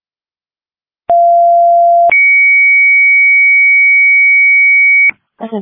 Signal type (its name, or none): two-tone page